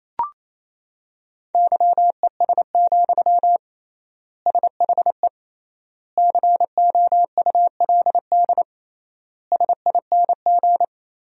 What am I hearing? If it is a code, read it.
Morse code, heard as a keyed tone, about 28 wpm, 700 Hz. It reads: YES, SHE COULD SING
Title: 3x Word-Spacing - Sentences from Top 500 Words - 28wpm